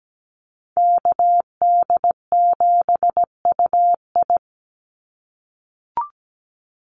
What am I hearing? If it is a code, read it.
Morse code, heard as a keyed tone, about 17 wpm, 700 Hz. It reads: KD7UI
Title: Call Signs - 17wpm